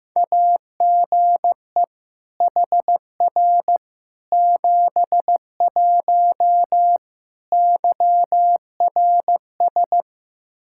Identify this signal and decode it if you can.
Morse code, heard as a keyed tone, about 15 wpm, 700 Hz. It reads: AGE HR 71 YRS
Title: Practice QSO - Exchange 3 - More Info - 15wpm